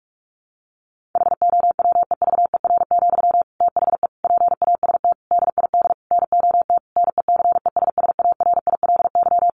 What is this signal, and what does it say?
Morse code, heard as a keyed tone, about 44 wpm, 700 Hz. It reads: HOWEVER, THE PAST DID NOT NECESSARILY
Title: Sentences from Top 1400 Words 45wpm